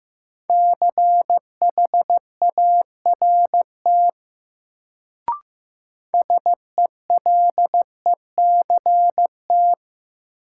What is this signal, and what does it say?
Morse code, heard as a keyed tone, about 15 wpm, 700 Hz. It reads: CHART SELECT
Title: Sentences from Top 1300 Words - New Word Drill - 15wpm